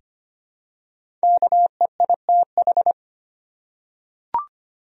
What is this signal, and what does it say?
Morse code, heard as a keyed tone, about 25 wpm, 700 Hz. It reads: KEITH
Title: USA Common Names - Max 5 Letters - 25wpm